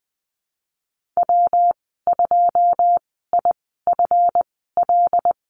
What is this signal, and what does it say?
Morse code, heard as a keyed tone, about 20 wpm, spaced as for 15 wpm, 700 Hz. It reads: W2IFL
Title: 154 Lesson 25g - Letter p - Call Signs 20wpm